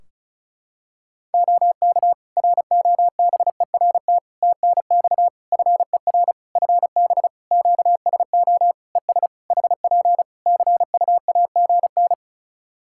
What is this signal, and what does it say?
Morse code, heard as a keyed tone, about 36 wpm, 700 Hz. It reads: OK ROBERT TNX FER FB QSO ES HP CUAGN
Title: K1USN SST QSO Practice 35wpm